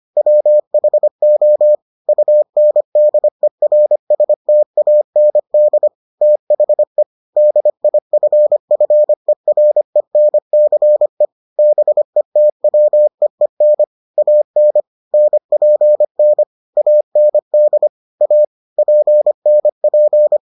Morse code, heard as a keyed tone, about 25 wpm, 600 Hz. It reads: WHO UNDERSTAND THE DIFFERENCE BETWEEN AN NPN AND A PNP